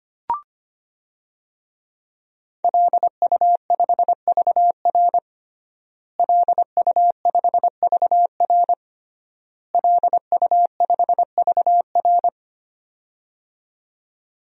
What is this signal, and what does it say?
Morse code, heard as a keyed tone, about 25 wpm, 700 Hz. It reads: LU5VR LU5VR LU5VR
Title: DXCC-100 Call Signs - 5 Characters - Repeated 3x - 25wpm